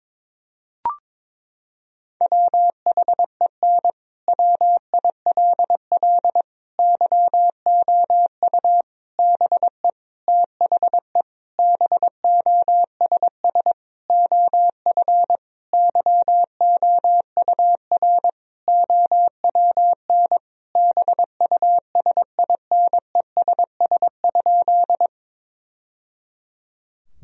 Morse code, heard as a keyed tone, about 22 wpm, 700 Hz. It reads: WHEN WILL YOU BE THE BOSS OF YOUR OWN BUSINESS?
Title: Sentences from Top 1000 Words - Review Part 4 - 22wpm